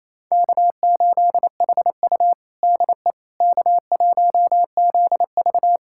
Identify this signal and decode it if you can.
Morse code, heard as a keyed tone, about 27 wpm, 700 Hz. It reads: K8HU DE K1ZV
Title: USA Callsign DE Callsign - 4 Characters - 28wpm